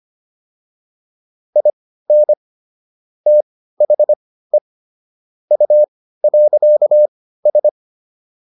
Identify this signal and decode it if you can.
Morse code, heard as a keyed tone, about 25 wpm, spaced as for 15 wpm, 600 Hz. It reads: IN THE U.S
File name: WD205 15